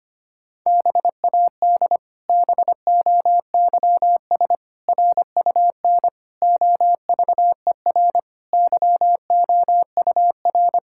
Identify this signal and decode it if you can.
Morse code, heard as a keyed tone, about 25 wpm, 700 Hz. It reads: BAD BOYS RUN OVER YOUR
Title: Pangram Challenge v2 - 25wpm